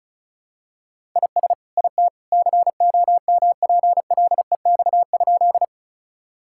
Morse code, heard as a keyed tone, about 36 wpm, 700 Hz. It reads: IS IT COMPLEX?
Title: Sentences from Top 1000 Words - Review Part 6 - 35wpm